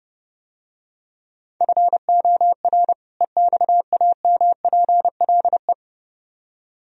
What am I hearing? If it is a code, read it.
Morse code, heard as a keyed tone, about 29 wpm, 700 Hz. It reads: FOR EXAMPLE